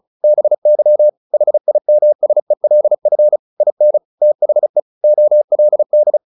Morse code, heard as a keyed tone, about 34 wpm, 600 Hz. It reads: BY HIMSELF IN THE OLD
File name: the old man and the sea / the old man and the sea 1 35wpm